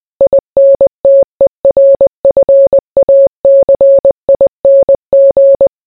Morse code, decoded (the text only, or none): INTERFACING